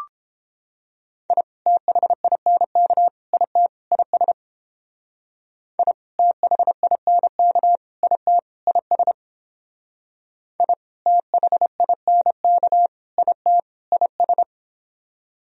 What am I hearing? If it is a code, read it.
Morse code, heard as a keyed tone, about 29 wpm, 700 Hz. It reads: I THINK IT IS I THINK IT IS I THINK IT IS